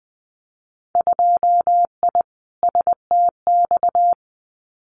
Morse code, heard as a keyed tone, about 20 wpm, 700 Hz. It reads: 2I STX
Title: Winter Field Day Exchanges 20wpm